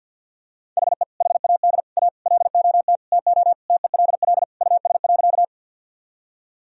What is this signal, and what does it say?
Morse code, heard as a keyed tone, about 46 wpm, 700 Hz. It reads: HE HAD A LOT TO TELL US.
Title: Sentences from Top 1000 Words - Review Part 3 - 50wpm